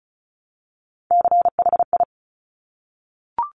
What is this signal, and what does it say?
Morse code, heard as a keyed tone, about 36 wpm, 700 Hz. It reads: CHI
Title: Top Three Letter Combinations in Words 35wpm